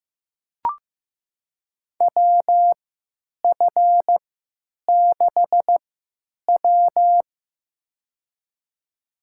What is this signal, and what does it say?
Morse code, heard as a keyed tone, about 15 wpm, spaced as for 9 wpm, 700 Hz. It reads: WF6W
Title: USA Call Signs - 4 Characters - 3x Character Spacing - 15wpm